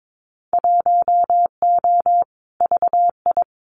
Morse code, heard as a keyed tone, about 22 wpm, 700 Hz. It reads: 1O VI
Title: Winter Field Day Exchanges Encore 22wpm